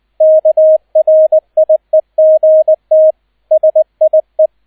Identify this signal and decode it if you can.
Morse code, heard as a keyed tone, about 20 wpm, 630 Hz, with radio noise. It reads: KRIEGT SIE
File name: Erster Streich 20wpm